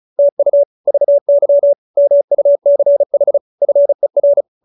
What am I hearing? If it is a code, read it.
Morse code, heard as a keyed tone, about 34 wpm, 570 Hz. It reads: TU VY MUCH FER